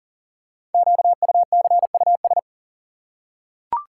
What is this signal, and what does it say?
Morse code, heard as a keyed tone, about 38 wpm, 690 Hz. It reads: QUCUS
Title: Code Groups - 5 Letters - 40wpm